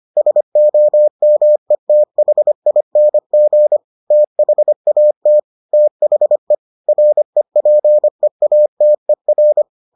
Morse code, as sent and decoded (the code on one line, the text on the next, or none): ... --- -- . - .... .. -. --. / - .... .- - / - .... / .-. .--. .- - . .-.
SOMETHING THAT THE REPEATER